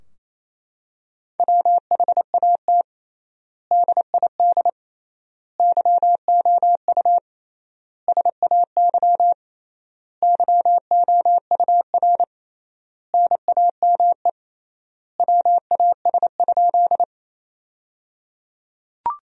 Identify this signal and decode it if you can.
Morse code, heard as a keyed tone, about 28 wpm, 700 Hz. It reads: WHAT DID YOU SAY YOUR NAME WAS?
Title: 3x Word-Spacing - Sentences from Top 200 Words - 28wpm